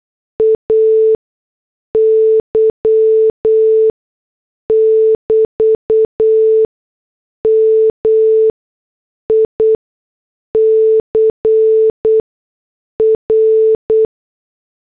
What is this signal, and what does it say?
Morse code, heard as a keyed tone, about 8 wpm, spaced as for 6 wpm, 440 Hz. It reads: AY <BT> MICR